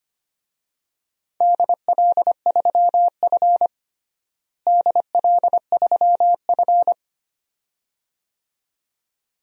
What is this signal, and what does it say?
Morse code, heard as a keyed tone, about 25 wpm, 700 Hz. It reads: DL3F DL3F
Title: DXCC-100 Call Signs - 4 Characters - Repeated 2x - 25wpm